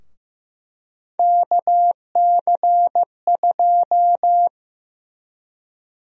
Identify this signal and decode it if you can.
Morse code, heard as a keyed tone, about 15 wpm, 700 Hz. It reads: KC2
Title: USA Call Sign Prefixes - Repeated 2x - 15wpm